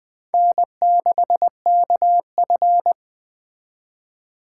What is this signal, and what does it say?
Morse code, heard as a keyed tone, about 20 wpm, 700 Hz. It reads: N6KF